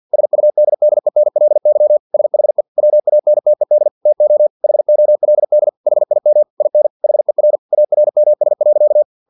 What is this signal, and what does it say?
Morse code, heard as a keyed tone, about 50 wpm, 600 Hz. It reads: SUDDENLY SHE WANTED TO HOLD HIM IN HER ARMS.
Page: Never 49wpm